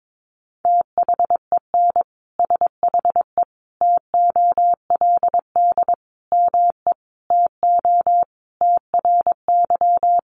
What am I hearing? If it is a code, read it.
Morse code, heard as a keyed tone, about 22 wpm, 700 Hz. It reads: THEN SHE TOLD ME TO TRY
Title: Sentences from Top 400 Words 22wpm